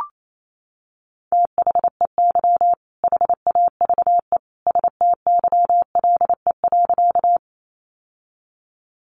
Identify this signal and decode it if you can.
Morse code, heard as a keyed tone, about 28 wpm, 700 Hz. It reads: THEY HAVE STYLE.
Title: Sentences from Top 800 Words 28wpm